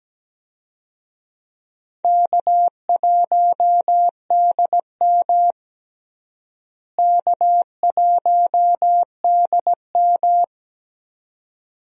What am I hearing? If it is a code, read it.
Morse code, heard as a keyed tone, about 17 wpm, 700 Hz. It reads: K1DM K1DM
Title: USA Call Signs - 4 Characters - Repeated 2x - 17wpm